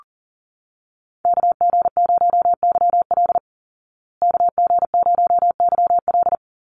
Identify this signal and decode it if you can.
Morse code, heard as a keyed tone, about 40 wpm, 700 Hz. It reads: KG0YL KG0YL
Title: Call Signs Repeated 2x - 40wpm